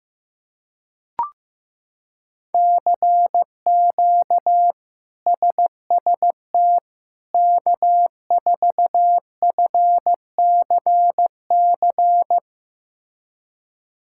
Morse code, heard as a keyed tone, about 15 wpm, 700 Hz. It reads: CQ SST K4FCC
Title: K1USN SST QSO Practice - 15wpm